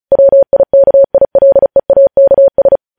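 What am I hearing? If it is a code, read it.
Morse code, heard as a keyed tone, about 36 wpm, 550 Hz. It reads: WIKILEAKS
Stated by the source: Morse code practice of news headlines at 35 wpm, opening with vvv.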